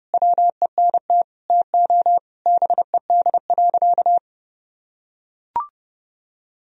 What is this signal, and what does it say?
Morse code, heard as a keyed tone, about 32 wpm, 700 Hz. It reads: WENT TO BED.
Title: Sentences from Top 1000 Words - Review Part 3 - 30wpm